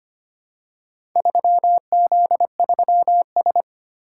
Morse code, heard as a keyed tone, about 25 wpm, 700 Hz. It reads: 3Z3S
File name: practice / 25wpm / Four Letter-Number Combinations - 25wpm